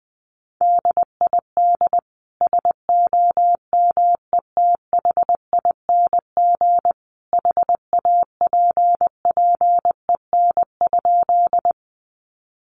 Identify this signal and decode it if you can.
Morse code, heard as a keyed tone, about 20 wpm, 700 Hz. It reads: DID SOMETHING HAPPEN?
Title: Sentences from Top 600 Words 20wpm